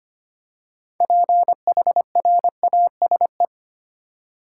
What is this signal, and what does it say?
Morse code, heard as a keyed tone, about 25 wpm, 700 Hz. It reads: PHRASE
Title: Six Letter Words - 25wpm